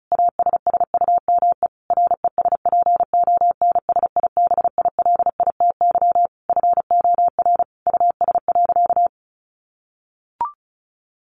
Morse code, heard as a keyed tone, about 34 wpm, 700 Hz. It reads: ASSUME RESPONSIBILITY FOR US.
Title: Sentences from Top 1500 Words 35wpm